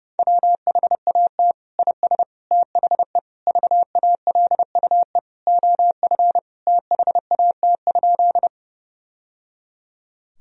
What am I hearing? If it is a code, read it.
Morse code, heard as a keyed tone, about 32 wpm, 700 Hz. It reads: WHAT IS THE VALUE OF THAT?
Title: Sentences from Top 1000 Words - Review Part 3 - 30wpm